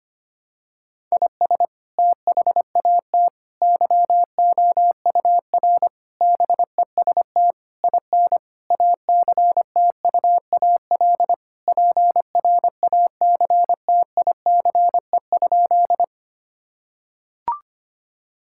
Morse code, heard as a keyed tone, about 25 wpm, 700 Hz. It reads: IS THAT YOUR BEST IN ACTUAL PRACTICE?
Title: Sentences from Top 1000 Words - Review Part 5 - 25wpm